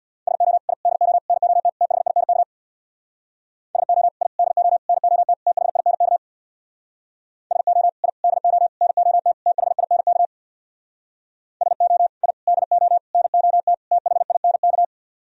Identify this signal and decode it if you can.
Morse code, heard as a keyed tone, about 62 wpm, 700 Hz. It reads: SO I DO NOT THINK SO I DO NOT THINK SO I DO NOT THINK SO I DO NOT THINK
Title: Speed-Racing - Sets of 5 Words 50wpm